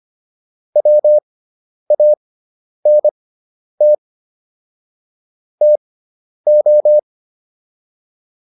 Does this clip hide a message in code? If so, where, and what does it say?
Yes: Morse code from 0.8 to 7.0 s, heard as a keyed tone, about 25 wpm, spaced as for 10 wpm, 600 Hz. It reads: WANT TO